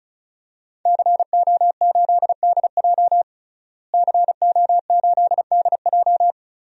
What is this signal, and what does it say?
Morse code, heard as a keyed tone, about 36 wpm, 700 Hz. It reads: CO8DJ CO8DJ
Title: DXCC-100 Call Signs - 5 Characters - Repeated 3x - 35wpm